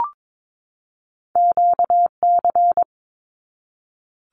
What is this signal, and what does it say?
Morse code, heard as a keyed tone, about 22 wpm, 700 Hz. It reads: QC
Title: Canadian Provinces and Territories 22wpm